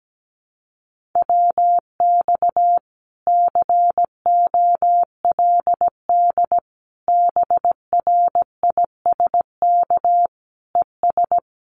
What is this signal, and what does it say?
Morse code, heard as a keyed tone, about 17 wpm, 700 Hz. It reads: WX COLD BRISK ES
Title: Common QSO Phrases Encore 17wpm